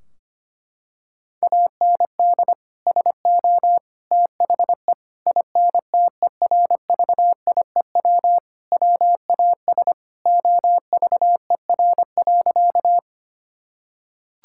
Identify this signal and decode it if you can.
Morse code, heard as a keyed tone, about 25 wpm, 700 Hz. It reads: AND SO THE INTERVIEW WAS OVER.
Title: Sentences from Top 1600 Words 25wpm